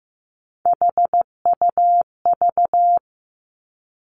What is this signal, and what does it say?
Morse code, heard as a keyed tone, about 15 wpm, 700 Hz. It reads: HUV
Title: Three Challenging Letter-Number Combinations 15wpm